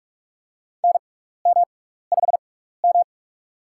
Morse code, heard as a keyed tone, about 43 wpm, spaced as for 15 wpm, 700 Hz. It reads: NM5M
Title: USA Call Signs - 4 Characters - 6x Character Spacing - 45wpm